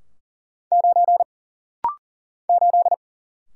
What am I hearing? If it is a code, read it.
Morse code, heard as a keyed tone, about 40 wpm, 700 Hz. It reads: 9 8
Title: Single Letter-Number - Rapid-Fire 40wpm